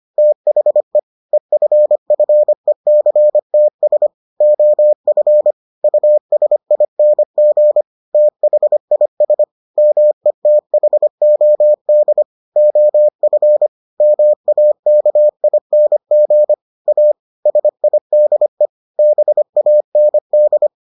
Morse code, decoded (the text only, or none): THE EFFECTS OF USING THIS METHOD OF MAKING A SIDE BAND